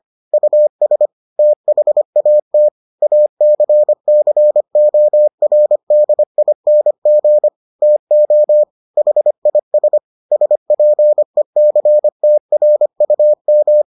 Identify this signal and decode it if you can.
Morse code, heard as a keyed tone, about 25 wpm, 600 Hz. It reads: US THAT ACCORDING TO HIS SPECTRUM